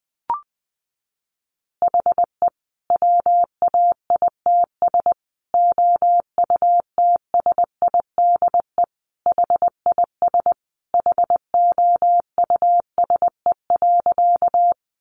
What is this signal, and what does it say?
Morse code, heard as a keyed tone, about 19 wpm, 700 Hz. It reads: HE WAITS OUTSIDE HIS HOUSE.
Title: Additional Easy Phrases 20wpm